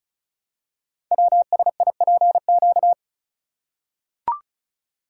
Morse code, heard as a keyed tone, about 36 wpm, 700 Hz. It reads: WSIPQ
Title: Code Groups - 5 Letters - 35wpm